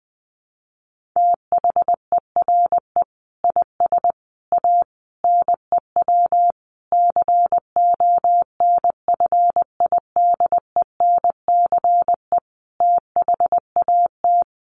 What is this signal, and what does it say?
Morse code, heard as a keyed tone, about 20 wpm, 700 Hz. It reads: THERE IS A NEW CONFIDENCE THAT
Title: Sentences from Top 1500 Words 20wpm